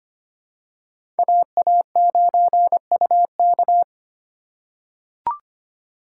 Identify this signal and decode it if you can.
Morse code, heard as a keyed tone, about 25 wpm, 700 Hz. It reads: AA9UK
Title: USA Call Signs - 5 Characters - 3x Character Spacing - 25wpm